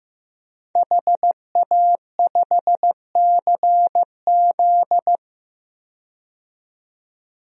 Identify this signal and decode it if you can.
Morse code, heard as a keyed tone, about 16 wpm, 700 Hz. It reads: HA5CZ
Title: DXCC-100 Call Signs - 5 Characters - 15wpm